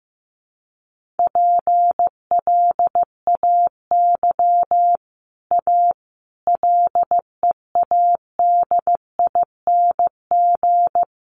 Morse code, heard as a keyed tone, about 15 wpm, 700 Hz. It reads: PLAY A LEADING
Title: Sentences from Top 1100 Words 15wpm